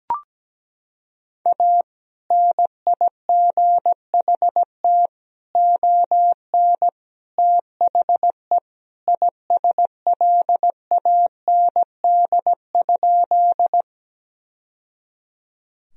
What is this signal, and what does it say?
Morse code, heard as a keyed tone, about 17 wpm, 700 Hz. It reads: A NIGHT ON THE ISLAND?
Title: Sentences from Top 1000 Words - Review Part 3 - 17wpm